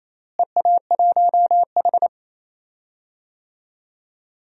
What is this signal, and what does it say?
Morse code, heard as a keyed tone, about 28 wpm, 700 Hz. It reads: EA1H
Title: DXCC-100 Call Signs - 4 Characters - Repeated 3x - 28wpm